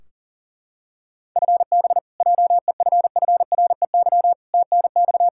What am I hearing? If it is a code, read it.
Morse code, heard as a keyed tone, about 40 wpm, 700 Hz. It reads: FB JEFFREY TNX
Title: Practice QSO - Exchange 3 - More Info - 40wpm